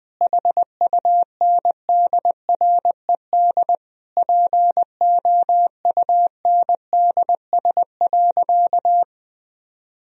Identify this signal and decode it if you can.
Morse code, heard as a keyed tone, about 20 wpm, 700 Hz. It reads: HUNDRED POUNDS.
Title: Sentences from Top 1000 Words - Review Part 2 - 20wpm